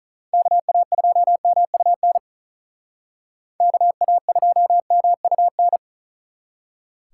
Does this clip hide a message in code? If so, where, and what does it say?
Yes: Morse code from 0.3 to 5.8 s, heard as a keyed tone, about 45 wpm, 700 Hz. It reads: KA2MUN KA2MUN